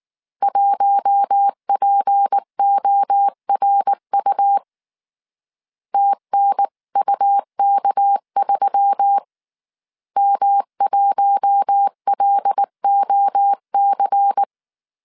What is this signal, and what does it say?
Morse code, heard as a keyed tone, about 19 wpm, 800 Hz. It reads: 1PORU TNUK3 M1LOC
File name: CW / DET / 1 6 1500C S C 18wpm 2356